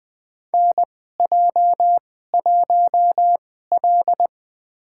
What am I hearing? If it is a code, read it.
Morse code, heard as a keyed tone, about 20 wpm, spaced as for 15 wpm, 700 Hz. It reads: NJ1L
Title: USA Call Signs - 4 Characters - 2x Character Spacing - 20wpm